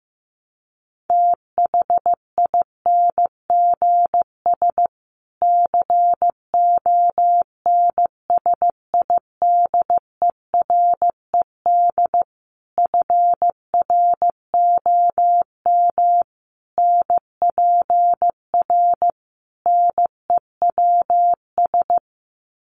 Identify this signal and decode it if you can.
Morse code, heard as a keyed tone, about 15 wpm, 700 Hz. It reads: THINGS CONSIDERED FROM NPR NEWS
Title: Sets of 5 Words Encore 15wpm